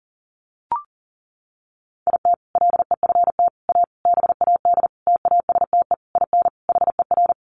Morse code, heard as a keyed tone, about 40 wpm, 700 Hz. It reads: IT LEFT A BAD TASTE IN HER
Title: Sentences from Top 1500 Words 40wpm